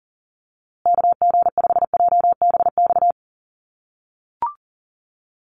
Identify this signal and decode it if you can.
Morse code, heard as a keyed tone, about 40 wpm, 700 Hz. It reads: KG5JBX
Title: USA Call Signs Repeated 3x 40wpm